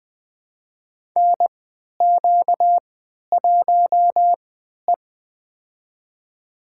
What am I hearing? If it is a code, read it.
Morse code, heard as a keyed tone, about 20 wpm, spaced as for 11 wpm, 700 Hz. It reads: NQ1E
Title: USA Call Signs - 4 Characters - 3x Character Spacing - 20wpm